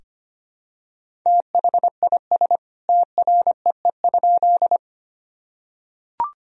Morse code, heard as a keyed tone, about 25 wpm, 700 Hz. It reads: THIS TREE?
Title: Sentences from Top 1000 Words - Review Part 1 - 25wpm